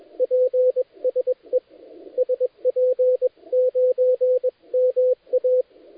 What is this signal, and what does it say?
Morse code, heard as a keyed tone, about 21 wpm, 510 Hz, with radio noise. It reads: PSE SP9MA
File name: dlazaawansowanych / nagraniazpasma / titawaNETcontrol